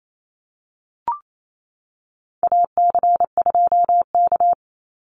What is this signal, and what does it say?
Morse code, heard as a keyed tone, about 28 wpm, 700 Hz. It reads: AC2K
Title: Call Signs - 28wpm